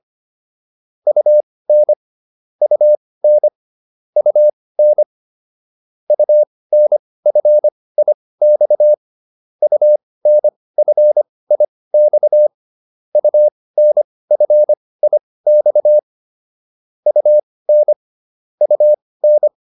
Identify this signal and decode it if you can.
Morse code, heard as a keyed tone, about 25 wpm, spaced as for 18 wpm, 600 Hz. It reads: UN UN UN UNFIX UNFIX UNFIX UN UN